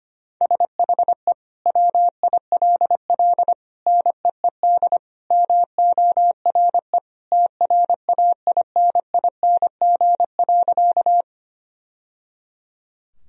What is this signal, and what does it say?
Morse code, heard as a keyed tone, about 25 wpm, 700 Hz. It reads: SHE WILL NEED MORE TRAINING.
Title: Sentences from Top 1000 Words - Review Part 4 - 25wpm